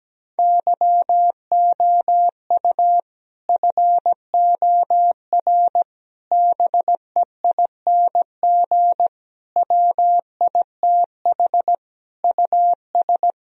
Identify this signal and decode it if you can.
Morse code, heard as a keyed tone, about 17 wpm, 700 Hz. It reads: YOU FOR BEING WITH US
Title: Speed-Racing - Sets of 5 Words 17wpm